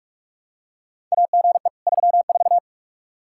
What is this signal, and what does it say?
Morse code, heard as a keyed tone, about 46 wpm, 700 Hz. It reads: AGE 34
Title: K1USN SST QSO Practice 45wpm